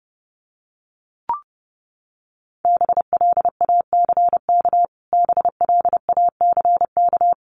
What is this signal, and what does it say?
Morse code, heard as a keyed tone, about 30 wpm, 700 Hz. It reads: BLACK BLACK
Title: Top 500 Words Repeated 30wpm